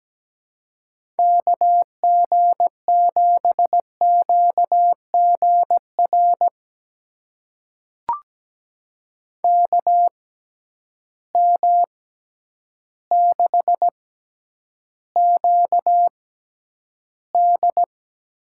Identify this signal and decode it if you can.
Morse code, heard as a keyed tone, about 17 wpm, spaced as for 9 wpm, 700 Hz. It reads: KG7QGR KM6QD
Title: USA Call Signs - 6 Characters - 6x Character Spacing - 17wpm